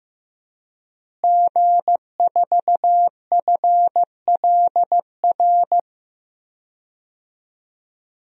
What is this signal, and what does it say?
Morse code, heard as a keyed tone, about 15 wpm, 700 Hz. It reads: G4FLR
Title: DXCC-100 Call Signs - 5 Characters - Repeated 3x - 15wpm